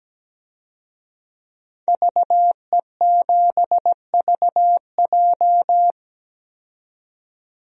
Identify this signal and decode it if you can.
Morse code, heard as a keyed tone, about 17 wpm, 700 Hz. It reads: VE7VJ
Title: DXCC-100 Call Signs - 5 Characters - 17wpm